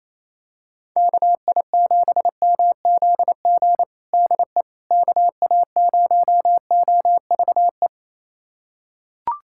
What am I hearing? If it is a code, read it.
Morse code, heard as a keyed tone, about 28 wpm, 700 Hz. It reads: KI7MZG DE KA0OVE
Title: USA Callsign DE Callsign 28wpm